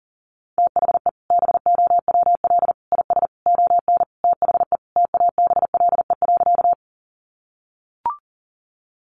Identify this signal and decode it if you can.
Morse code, heard as a keyed tone, about 38 wpm, 700 Hz. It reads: THE BOWL IS ON THE TABLE.
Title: Sentences from Top 1200 Words 40wpm